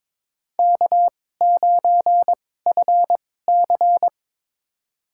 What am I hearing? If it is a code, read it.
Morse code, heard as a keyed tone, about 23 wpm, spaced as for 16 wpm, 700 Hz. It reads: K9FC
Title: USA Call Signs - 2x Character Spacing - 22wpm